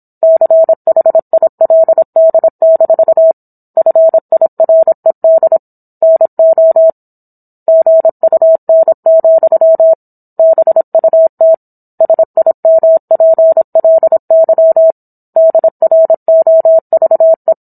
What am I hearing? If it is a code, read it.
Morse code, heard as a keyed tone, about 27 wpm, 650 Hz. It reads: CHILD- FIRED NO GUN, BUT SIMPLY DROVE
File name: Book One / 17-Chapter 17 - 26 WPM